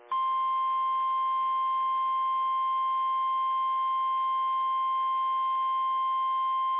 Emergency Alert System (SAME) tones (the 1050 Hz attention tone). It is heard from 1.0 to 6.3 s.
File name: eas / KIMB / KIMB